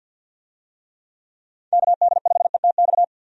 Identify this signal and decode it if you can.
Morse code, heard as a keyed tone, about 47 wpm, 700 Hz. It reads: KD5ETX
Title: USA Call Signs - 6 Characters - 50wpm